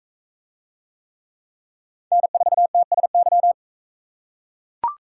Morse code, heard as a keyed tone, about 38 wpm, 700 Hz. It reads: N4TSY